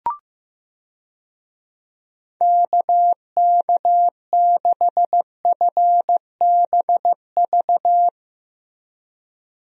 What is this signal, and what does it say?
Morse code, heard as a keyed tone, about 15 wpm, 700 Hz. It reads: KK6FBV
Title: USA Call Signs - 6 Characters - Repeated 3x - 15wpm